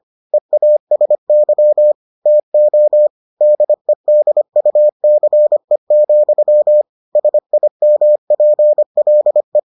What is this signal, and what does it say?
Morse code, heard as a keyed tone, about 25 wpm, 600 Hz. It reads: EASY TO DEDUCE, SIMPLE